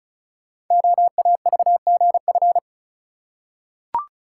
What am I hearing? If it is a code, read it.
Morse code, heard as a keyed tone, about 36 wpm, 690 Hz. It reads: OAVGF
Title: Code Groups - 5 Letters - 35wpm